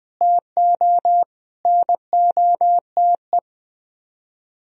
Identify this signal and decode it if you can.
Morse code, heard as a keyed tone, about 20 wpm, 700 Hz. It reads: TO NOTE
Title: Speed-Racing - Sets of 5 Words 17wpm